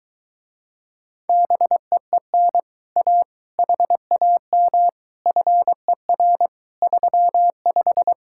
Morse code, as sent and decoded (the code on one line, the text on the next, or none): -... . -. / .- / .... .- -- / ..-. .-. / ...-- .....
BEEN A HAM FER 35